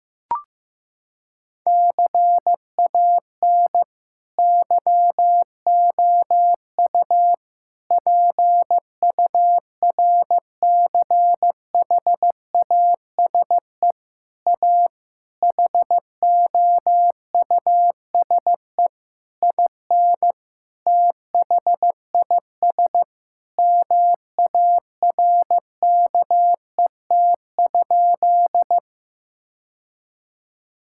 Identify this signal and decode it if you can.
Morse code, heard as a keyed tone, about 15 wpm, 700 Hz. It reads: CAN YOU PURCHASE A HOUSE IN THIS MARKET?
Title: Sentences from Top 1000 Words - Review Part 5 - 15wpm